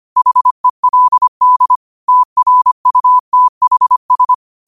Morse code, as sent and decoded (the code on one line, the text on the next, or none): .... .-.. -.. / - .-. ..- - .... ...
HELD TRUTHS